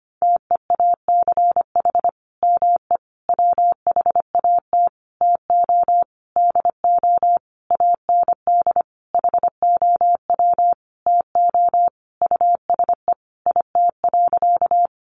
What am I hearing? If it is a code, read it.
Morse code, heard as a keyed tone, about 25 wpm, 700 Hz. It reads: TEACH ME WHAT TO DO AND HOW TO USE IT.
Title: Sentences from Top 500 Words 25wpm